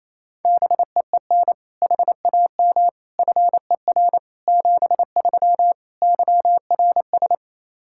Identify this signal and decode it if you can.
Morse code, heard as a keyed tone, about 28 wpm, 700 Hz. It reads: BEEN HAM FER 73 YRS
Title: Practice QSO - Exchange 3 - More Info - 28wpm